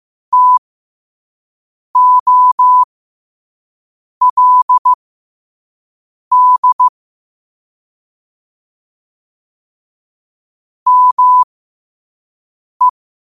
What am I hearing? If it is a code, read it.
Morse code, heard as a keyed tone, about 15 wpm, spaced as for 5 wpm, 1000 Hz. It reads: TOLD ME
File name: qotdfiles / 5wpm